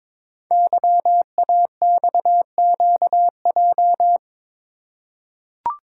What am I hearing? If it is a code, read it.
Morse code, heard as a keyed tone, about 22 wpm, 700 Hz. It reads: YAXQJ
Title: Code Groups - 5 Letters - 22wpm